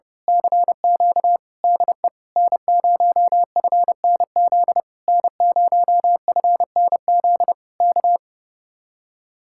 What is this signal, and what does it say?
Morse code, heard as a keyed tone, about 29 wpm, 700 Hz. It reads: CQ DE N0FNZ N0FNZ K
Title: Speed-Racing Common QSO Phrases - 30wpm